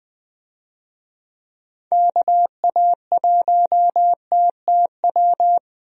Morse code, heard as a keyed tone, about 20 wpm, 700 Hz. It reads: KA1TTW
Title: USA Call Signs - 6 Characters - 20wpm